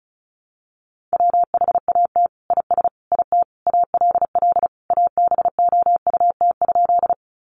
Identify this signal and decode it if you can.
Morse code, heard as a keyed tone, about 34 wpm, 700 Hz. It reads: WHAT IS IT ALL ABOUT?
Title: Sentences from Top 100 Words 35wpm